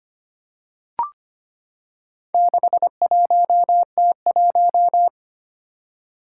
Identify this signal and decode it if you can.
Morse code, heard as a keyed tone, about 25 wpm, 700 Hz. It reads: 61T1
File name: practice / 25wpm / Four Letter-Number Combinations - 25wpm